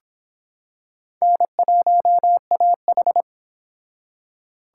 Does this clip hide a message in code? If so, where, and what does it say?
Yes: Morse code from 1.2 to 3.2 s, heard as a keyed tone, about 26 wpm, 700 Hz. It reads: N1AH